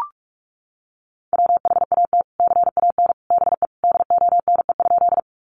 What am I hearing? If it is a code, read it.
Morse code, heard as a keyed tone, about 44 wpm, 700 Hz. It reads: WHAT CAN BE DONE?
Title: Sentences from Top 500 Words 45wpm